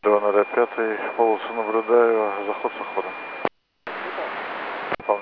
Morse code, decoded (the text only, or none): none